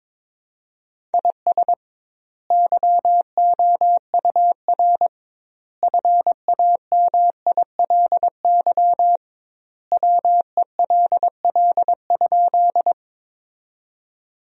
Morse code, heard as a keyed tone, about 21 wpm, 700 Hz. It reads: IS YOUR FAMILY WELL?
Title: 2x Word-Spacing - Sentences from Top 400 Words - 22wpm